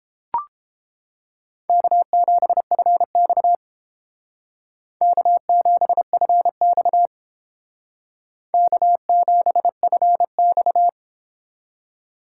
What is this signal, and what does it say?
Morse code, heard as a keyed tone, about 29 wpm, 700 Hz. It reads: K7FX K7FX K7FX